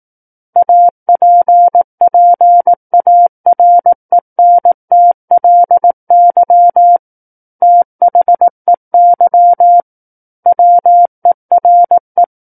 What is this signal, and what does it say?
Morse code, heard as a keyed tone, about 18 wpm, 700 Hz. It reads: APPARENTLY THEY WERE
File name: Book One / 09-Chapter 9 - 18 WPM